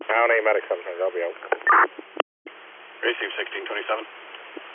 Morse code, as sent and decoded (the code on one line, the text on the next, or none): none